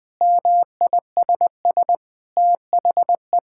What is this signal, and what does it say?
Morse code, heard as a keyed tone, about 20 wpm, 700 Hz. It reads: MISS THE
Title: Sentences with 8-Word Phrases - 20wpm